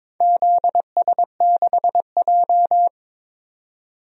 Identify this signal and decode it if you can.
Morse code, heard as a keyed tone, about 23 wpm, 700 Hz. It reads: ZS6J
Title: DXCC-100 Call Signs - 4 Characters - 22wpm